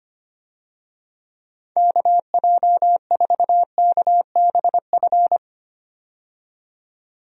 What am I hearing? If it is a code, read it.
Morse code, heard as a keyed tone, about 25 wpm, 700 Hz. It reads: KJ4KBF
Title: USA Call Signs - 6 Characters - 25wpm